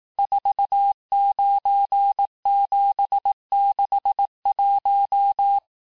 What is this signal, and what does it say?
Morse code, heard as a keyed tone, about 18 wpm, 800 Hz. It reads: 49761
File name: cours-du-lundi files / cours-1588-texte-chiffre-vitesse-17MPM